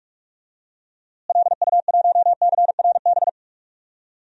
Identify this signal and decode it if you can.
Morse code, heard as a keyed tone, about 42 wpm, 700 Hz. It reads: LU1CRB
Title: DXCC-100 Call Signs - 6 Characters - Repeated 2x - 45wpm